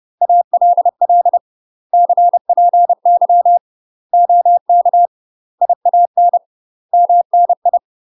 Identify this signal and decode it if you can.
Morse code, heard as a keyed tone, about 30 wpm, 700 Hz. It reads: ALL CPY OK IAN MNI
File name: UK MASTER QSOs TRAINING / 30WPM / UK MASTER QSOs 0003